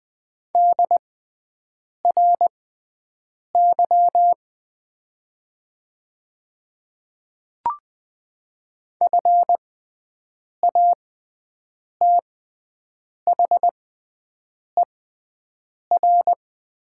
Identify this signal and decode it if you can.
Morse code, heard as a keyed tone, about 20 wpm, spaced as for 7 wpm, 700 Hz. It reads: DRY FATHER
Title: 6x Character Spacing - Top 500 Words - 20wpm